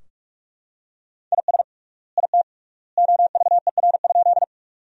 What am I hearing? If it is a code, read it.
Morse code, heard as a keyed tone, about 41 wpm, 700 Hz. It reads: IS IT OVER?
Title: 3x Word-Spacing - Sentences from Top 100 Words - 45wpm